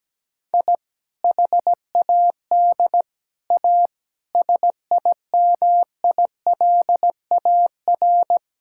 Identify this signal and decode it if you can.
Morse code, heard as a keyed tone, about 17 wpm, 700 Hz. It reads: I HAD A SIMILAR
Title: Sentences from Top 1000 Words - Review Part 5 - 17wpm